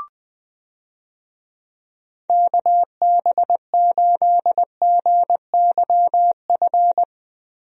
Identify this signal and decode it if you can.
Morse code, heard as a keyed tone, about 20 wpm, 700 Hz. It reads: KB8GYF
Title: USA Call Signs - 6 Characters - Repeated 3x - 20wpm